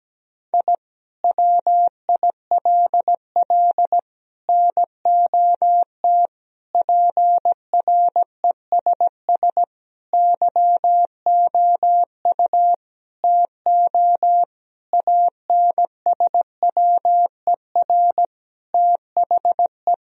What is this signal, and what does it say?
Morse code, heard as a keyed tone, about 17 wpm, 700 Hz. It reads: I WILL NOT PRESS YOU TO ANSWER THE
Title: Sentences from Top 1000 Words - Review Part 1 - 17wpm